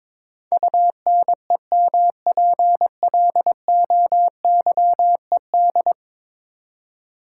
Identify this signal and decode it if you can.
Morse code, heard as a keyed tone, about 21 wpm, 700 Hz. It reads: UNEMPLOYED
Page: morse-code-ninja-practice-22wpm